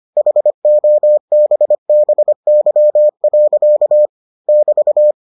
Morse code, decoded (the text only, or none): HOBBY. <BT>